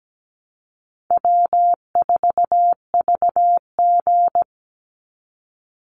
Morse code, heard as a keyed tone, about 17 wpm, 700 Hz. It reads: W4VG
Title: Easy Call Signs Repeated 3x - 17wpm